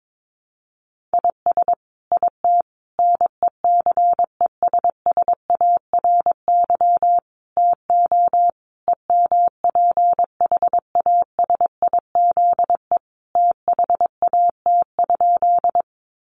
Morse code, heard as a keyed tone, about 22 wpm, 700 Hz. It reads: IS IT NECESSARY TO EMPHASIZE THAT?
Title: Sentences from Top 1500 Words 22wpm